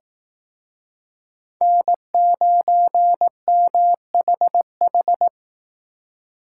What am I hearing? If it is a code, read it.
Morse code, heard as a keyed tone, about 18 wpm, 700 Hz. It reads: N9MHH
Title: Speed Racing - USA 1x3 Call Signs 15wpm